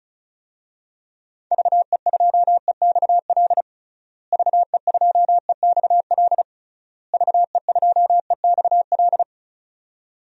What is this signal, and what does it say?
Morse code, heard as a keyed tone, about 34 wpm, 700 Hz. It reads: VE2EXL VE2EXL VE2EXL
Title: DXCC-100 Call Signs - 6 Characters - Repeated 3x - 35wpm